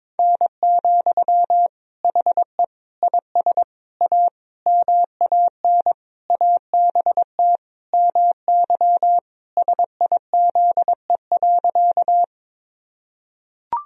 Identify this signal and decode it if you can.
Morse code, heard as a keyed tone, about 21 wpm, 700 Hz. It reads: N, HE IS A MAN ABT MY SIZE.
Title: Sentences from Top 400 Words and Abbreviations 22wpm